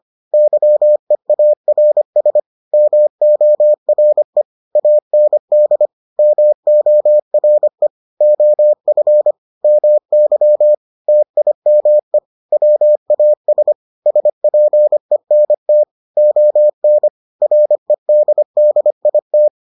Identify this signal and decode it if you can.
Morse code, heard as a keyed tone, about 25 wpm, 600 Hz. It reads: YEARS MORE AND MORE OF MY TIME WAS SPENT ON REDDIT